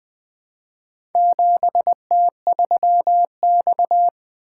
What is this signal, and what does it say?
Morse code, heard as a keyed tone, about 20 wpm, 700 Hz. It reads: 7T3X
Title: Four Letter-Number Combinations - 20wpm